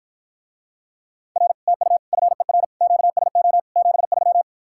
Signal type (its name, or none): Morse code